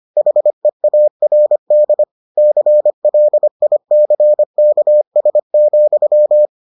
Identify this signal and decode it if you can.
Morse code, heard as a keyed tone, about 25 wpm, 600 Hz. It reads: HEARD CLICKS,